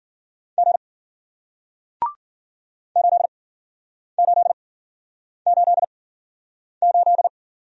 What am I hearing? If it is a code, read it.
Morse code, heard as a keyed tone, about 56 wpm, spaced as for 9 wpm, 700 Hz. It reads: G 8888